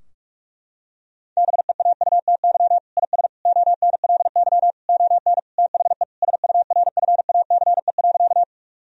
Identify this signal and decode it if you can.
Morse code, heard as a keyed tone, about 46 wpm, 700 Hz. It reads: BEAUTY IS ONLY ON THE SURFACE.
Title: Sentences from Top 1000 Words - Review Part 3 - 45wpm